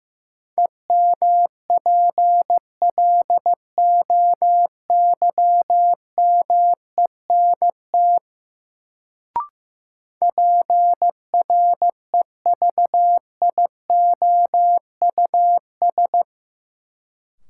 Morse code, heard as a keyed tone, about 15 wpm, 700 Hz. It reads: EMPLOYMENT PREVIOUS
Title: Sentences from Top 1500 Words - New Word Drill Alt - 15wpm